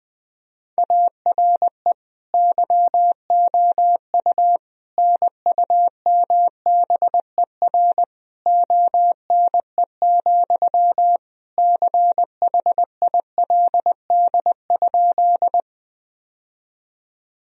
Morse code, heard as a keyed tone, about 19 wpm, 700 Hz. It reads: ARE YOU NUMBER ONE, CHILD?
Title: Sentences from Top 1000 Words - Review Part 1 - 20wpm